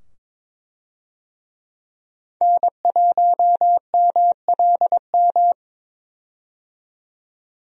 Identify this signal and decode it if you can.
Morse code, heard as a keyed tone, about 22 wpm, 700 Hz. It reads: N1MLM
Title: USA Call Signs - 5 Characters - 22wpm